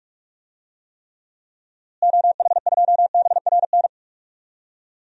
Morse code, heard as a keyed tone, about 45 wpm, 700 Hz. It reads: OH2BRN
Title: DXCC-100 Call Signs - 6 Characters - Repeated 3x - 45wpm